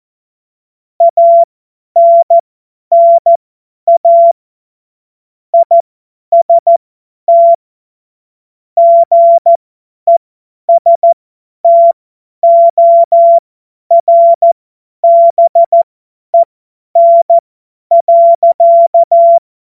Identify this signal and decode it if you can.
Morse code, heard as a keyed tone, about 14 wpm, spaced as for 10 wpm, 680 Hz. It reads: ANNA IST GESTORBEN.
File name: Morsekurs / Witze / Witz 08. Februar 2022 50 F70